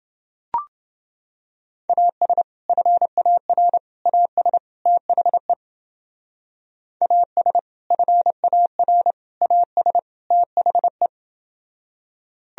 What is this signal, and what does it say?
Morse code, heard as a keyed tone, about 31 wpm, 700 Hz. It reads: AS FAR AS THE AS FAR AS THE